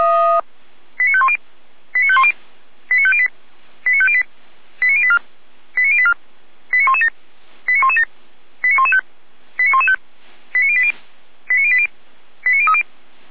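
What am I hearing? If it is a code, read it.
Five-tone selective call (ZVEI), heard as quick runs of five tones, tones of about 70 ms. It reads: 89510 89510 89508 89508 89084 89084 89117 89117 89115 89115 89080 89080 89030
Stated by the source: ZVEI five-tone selective call received by SDR.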